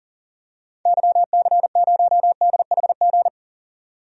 Morse code, heard as a keyed tone, about 38 wpm, 700 Hz. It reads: YC0DHG
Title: DXCC-100 Call Signs - 6 Characters - 40wpm